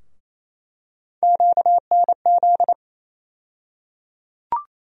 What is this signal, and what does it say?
Morse code, heard as a keyed tone, about 28 wpm, 700 Hz. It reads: QNZ
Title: USA Call Sign Postfixes - 3 Letters 28wpm